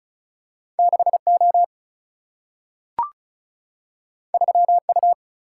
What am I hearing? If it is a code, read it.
Morse code, heard as a keyed tone, about 36 wpm, 700 Hz. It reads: 6O 3U
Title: Single-Letter Single-Number 35wpm